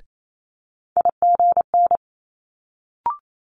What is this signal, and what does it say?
Morse code, heard as a keyed tone, about 28 wpm, 700 Hz. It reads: IGN
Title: Multi-letter Phonemes Quick Fire 28wpm